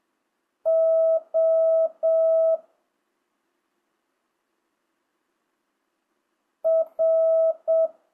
Morse code, heard as a keyed tone, about 7 wpm, 650 Hz. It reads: O R
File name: jh8cbh / 2025cw